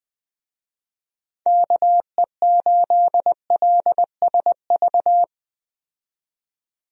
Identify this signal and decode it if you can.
Morse code, heard as a keyed tone, about 20 wpm, 700 Hz. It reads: KE8LSV
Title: USA Call Signs - 6 Characters - 20wpm